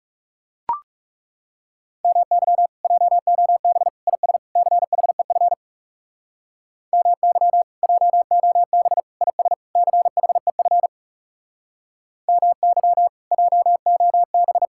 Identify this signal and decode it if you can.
Morse code, heard as a keyed tone, about 40 wpm, 700 Hz. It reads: MY JOB IS CHEF MY JOB IS CHEF MY JOB